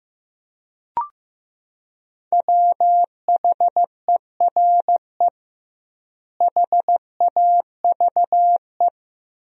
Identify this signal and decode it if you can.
Morse code, heard as a keyed tone, about 15 wpm, 700 Hz. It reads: WHERE HAVE
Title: 2x Word-Spacing - Sentences from Top 300 Words - 15wpm